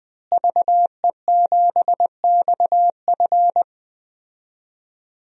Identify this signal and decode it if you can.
Morse code, heard as a keyed tone, about 20 wpm, 700 Hz. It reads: VE7XF
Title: DXCC-100 Call Signs - 5 Characters - Repeated 2x - 20wpm